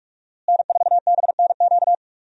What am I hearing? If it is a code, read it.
Morse code, heard as a keyed tone, about 46 wpm, 700 Hz. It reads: N4BNQ
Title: USA Call Signs - 5 Characters - Repeated 2x - 45wpm